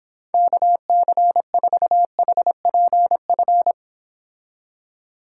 Morse code, heard as a keyed tone, about 26 wpm, 700 Hz. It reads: KC4HPF